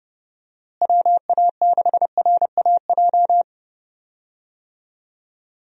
Morse code, heard as a keyed tone, about 30 wpm, 700 Hz. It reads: WA6RAJ
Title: USA Call Signs - 6 Characters - Repeated 2x - 30wpm